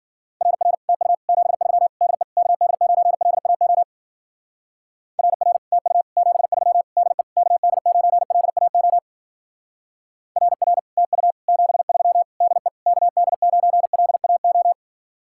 Morse code, heard as a keyed tone, about 49 wpm, 700 Hz. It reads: RR TU 73 DE KD9LAO RR TU 73 DE KD9LAO RR TU 73 DE KD9LAO
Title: Speed-Racing Common QSO Phrases Encore - 40wpm